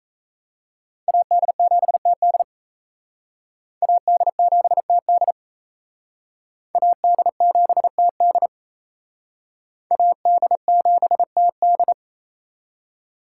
Morse code, heard as a keyed tone, about 35 wpm, 700 Hz. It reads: AD7TD AD7TD AD7TD AD7TD
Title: Speed Racing - USA 2x2 Call Signs 28wpm